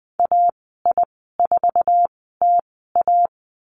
Morse code, heard as a keyed tone, about 20 wpm, spaced as for 15 wpm, 700 Hz. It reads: AI4TA
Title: 055 Lesson 10g - Letter r - Call Signs 20wpm